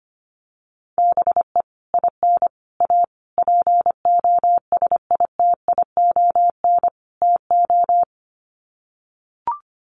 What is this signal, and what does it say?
Morse code, heard as a keyed tone, about 25 wpm, 700 Hz. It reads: BE IN A POSITION TO
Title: Sets of 5 Words Encore2 25wpm